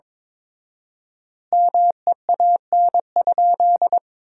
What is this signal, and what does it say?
Morse code, heard as a keyed tone, about 23 wpm, 700 Hz. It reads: MEAN?
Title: 4x Word-Spacing - Sentences from Top 500 Words - 22wpm